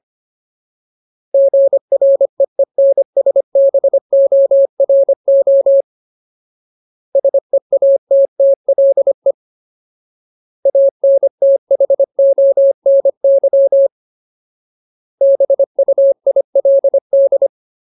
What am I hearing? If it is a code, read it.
Morse code, heard as a keyed tone, about 25 wpm, 550 Hz. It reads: GREENSBORO SEATTLE ANTHONY BUILD
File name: Ultimate Word List / 25WPM Ultimate